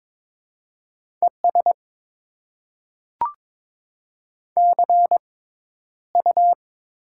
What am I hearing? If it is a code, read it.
Morse code, heard as a keyed tone, about 22 wpm, spaced as for 11 wpm, 700 Hz. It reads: ES CU
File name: Top 20 QSO Elements - 6x Character Spacing - 22wpm